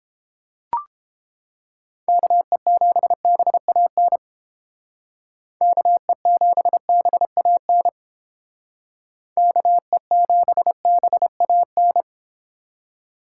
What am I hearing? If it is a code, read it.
Morse code, heard as a keyed tone, about 29 wpm, 700 Hz. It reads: KE7BAN KE7BAN KE7BAN